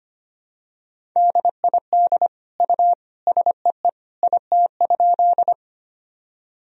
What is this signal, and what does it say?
Morse code, heard as a keyed tone, about 25 wpm, 700 Hz. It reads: DID U SEE IT?
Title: Sentences from Top 100 Words and Abbreviations 25wpm